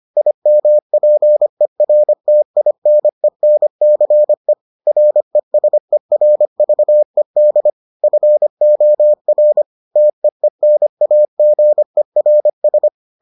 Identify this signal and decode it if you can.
Morse code, heard as a keyed tone, about 25 wpm, 600 Hz. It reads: IMPERTINENCE RESERVED FOR TEENAGERS